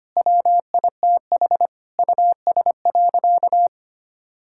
Morse code, heard as a keyed tone, about 25 wpm, 700 Hz. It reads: WITH US.
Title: Sentences from Top 1000 Words - Review Part 6 - 25wpm